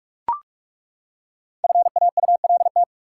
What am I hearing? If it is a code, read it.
Morse code, heard as a keyed tone, about 42 wpm, 700 Hz. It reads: FAULT
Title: Five Letter Words - 45wpm